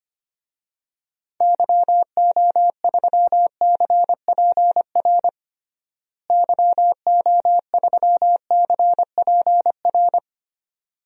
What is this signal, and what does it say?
Morse code, heard as a keyed tone, about 25 wpm, 700 Hz. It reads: YO3CPR YO3CPR
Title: DXCC-100 Call Signs - 6 Characters - Repeated 2x - 25wpm